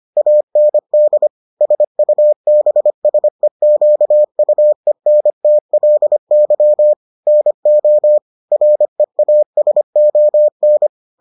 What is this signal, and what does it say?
Morse code, heard as a keyed tone, about 25 wpm, 600 Hz. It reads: AND SUBSEQUENTLY NO REASON